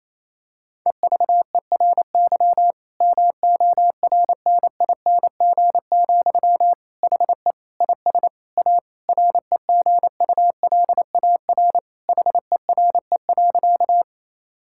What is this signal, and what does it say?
Morse code, heard as a keyed tone, about 27 wpm, 690 Hz. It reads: EVERY MORNING, HE IS A REGULAR HERE.
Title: Sentences from Top 1000 Words - Review Part 6 - 28wpm